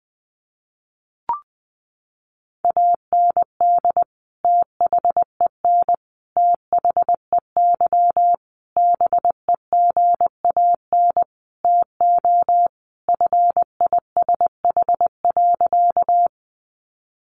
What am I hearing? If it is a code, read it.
Morse code, heard as a keyed tone, about 20 wpm, 700 Hz. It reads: AND THEN THEY BEGAN TO FISH.
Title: Sentences from Top 300 Words 20wpm